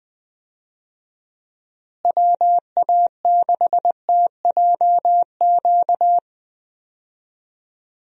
Morse code, heard as a keyed tone, about 20 wpm, 700 Hz. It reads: WA6TJQ
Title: USA Call Signs - 6 Characters - Repeated 3x - 20wpm